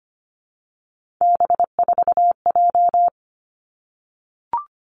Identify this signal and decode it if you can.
Morse code, heard as a keyed tone, about 25 wpm, 700 Hz. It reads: B4J
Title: Three Challenging Letter-Number Combinations 25wpm